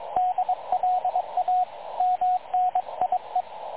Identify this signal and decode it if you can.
Morse code, heard as a keyed tone, about 22 wpm, 730 Hz, with radio noise. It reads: DLA MNIE
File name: dlazaawansowanych / nagraniazpasma / websdr recording 3567.0kHz